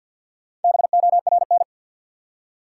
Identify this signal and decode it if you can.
Morse code, heard as a keyed tone, about 47 wpm, 700 Hz. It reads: BORN
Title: Four Letter Words - 50wpm